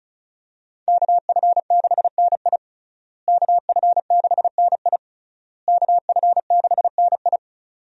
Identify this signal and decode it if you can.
Morse code, heard as a keyed tone, about 36 wpm, 700 Hz. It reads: KF6NI KF6NI KF6NI
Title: USA Call Signs - 5 Characters - Repeated 3x - 35wpm